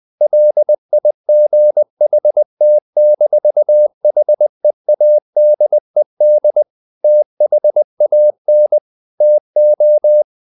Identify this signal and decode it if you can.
Morse code, heard as a keyed tone, about 20 wpm, 600 Hz. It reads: LIGHT-HEADED THAN TO
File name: the old man and the sea part 2 20wpm